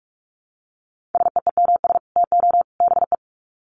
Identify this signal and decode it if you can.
Morse code, heard as a keyed tone, about 44 wpm, 700 Hz. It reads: SEEMS TO BE
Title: Sets of 3 Words 45wpm